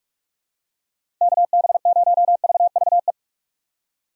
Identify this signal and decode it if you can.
Morse code, heard as a keyed tone, about 46 wpm, 700 Hz. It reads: KB0VVE
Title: USA Call Signs - 6 Characters - 45wpm